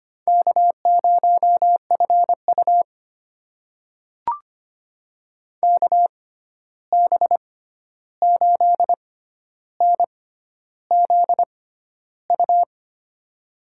Morse code, heard as a keyed tone, about 25 wpm, spaced as for 11 wpm, 700 Hz. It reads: K0FU KB8NZU
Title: USA Call Signs - 6x Character Spacing - 25wpm